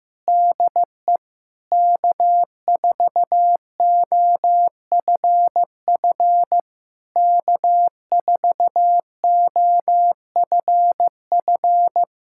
Morse code, heard as a keyed tone, about 15 wpm, 700 Hz. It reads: DE K4OFF K4OFF
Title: Speed-Racing Common QSO Phrases Encore - 15wpm